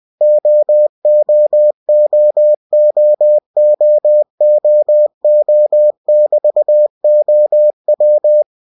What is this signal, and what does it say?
Morse code, heard as a keyed tone, about 20 wpm, 600 Hz. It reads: OOOOOOO<BT>OW
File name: cw-books / winnie-the-pooh / winnie-20wpm-page-0070